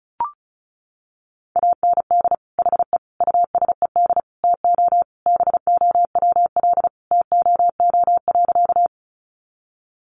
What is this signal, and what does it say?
Morse code, heard as a keyed tone, about 34 wpm, 700 Hz. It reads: AND HE USED TO BOWL TOO.
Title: Sentences from Top 1200 Words 35wpm